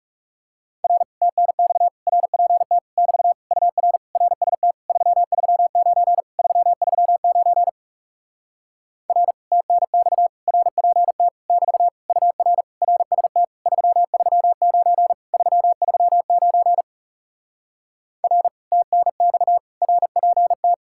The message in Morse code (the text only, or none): R TNX RPT <BT> UR RST 339 339 R TNX RPT <BT> UR RST 339 339 R TNX RPT